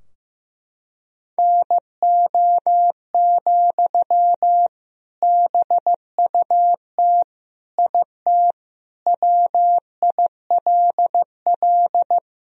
Morse code, heard as a keyed tone, about 15 wpm, 700 Hz. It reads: NO, BUT IT WILL
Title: Sentences from Top 1000 Words - Review Part 2 - 15wpm